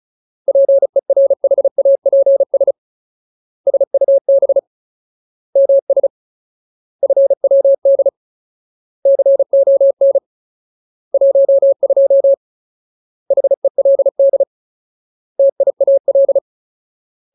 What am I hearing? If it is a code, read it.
Morse code, heard as a keyed tone, about 34 wpm, 550 Hz. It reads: PERHAPS SUB MS FWD CON 12 HELD TIAL
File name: Ultimate Word List / 35WPM Ultimate